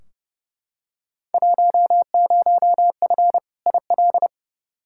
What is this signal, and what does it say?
Morse code, heard as a keyed tone, about 32 wpm, 700 Hz. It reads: 10F IL
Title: Field Day Exchanges Encore - 30wpm